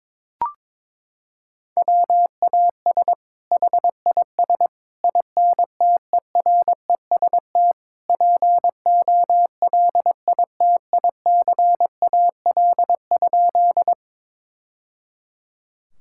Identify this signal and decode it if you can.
Morse code, heard as a keyed tone, about 22 wpm, 700 Hz. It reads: WAS HIS INTEREST POLITICAL?
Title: Sentences from Top 1000 Words - Review Part 5 - 22wpm